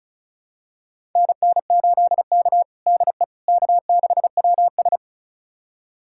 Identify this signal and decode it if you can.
Morse code, heard as a keyed tone, about 36 wpm, 700 Hz. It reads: NN8K DE K6WS
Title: USA Callsign DE Callsign - 4 Characters - 35wpm